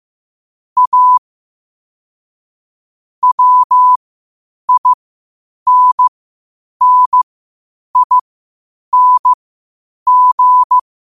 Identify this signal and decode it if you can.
Morse code, heard as a keyed tone, about 15 wpm, spaced as for 8 wpm, 1000 Hz. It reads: A WINNING
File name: qotdfiles / 8wpm